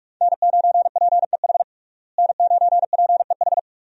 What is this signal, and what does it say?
Morse code, heard as a keyed tone, about 46 wpm, 700 Hz. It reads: N9PEH N9PEH
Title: USA Call Signs - 5 Characters - Repeated 2x - 45wpm